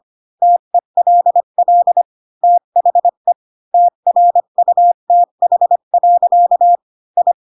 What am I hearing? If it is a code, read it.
Morse code, heard as a keyed tone, about 25 wpm, 700 Hz. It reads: TELL THE TRUTH. I
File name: Pinocchio / Pinocchio ch8 20 25 ik1hge